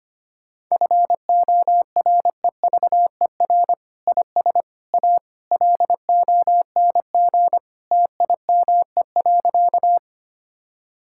Morse code, heard as a keyed tone, about 25 wpm, 700 Hz. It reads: FOREVER IS A LONG TIME.
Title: Sentences from Top 1600 Words 25wpm